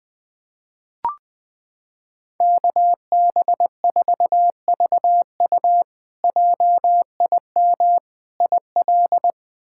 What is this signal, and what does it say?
Morse code, heard as a keyed tone, about 20 wpm, 700 Hz. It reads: KB4VU JIM IL
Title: K1USN SST QSO Practice - 20wpm